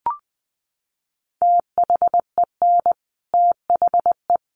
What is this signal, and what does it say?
Morse code, heard as a keyed tone, about 20 wpm, 700 Hz. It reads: THEN THE
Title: Sentences from Top 1500 Words 20wpm